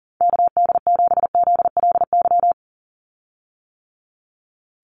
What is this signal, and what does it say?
Morse code, heard as a keyed tone, about 40 wpm, 700 Hz. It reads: KD7ZLY
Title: USA Call Signs 40wpm